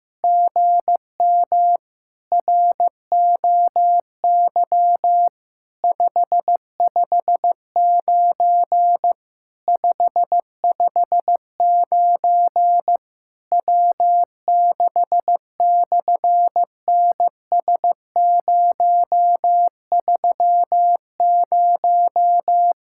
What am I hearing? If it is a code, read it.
Morse code, heard as a keyed tone, about 14 wpm, 700 Hz. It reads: GM ROY 559 559 W6/NS030
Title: SOTA QSO Practice 15wpm